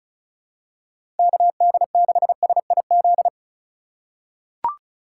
Morse code, heard as a keyed tone, about 34 wpm, 700 Hz. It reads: KD6SIZ
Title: USA Call Signs - 4x Character Spacing - 35wpm